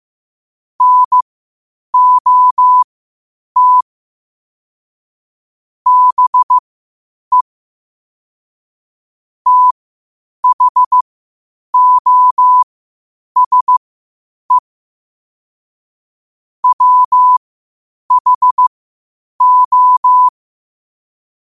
Morse code, heard as a keyed tone, about 15 wpm, spaced as for 8 wpm, 1000 Hz. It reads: NOT BE THOSE WHO
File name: 8wpm